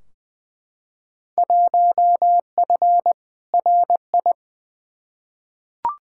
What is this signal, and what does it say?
Morse code, heard as a keyed tone, about 20 wpm, 700 Hz. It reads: 1F RI
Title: Field Day Exchanges Encore - 20wpm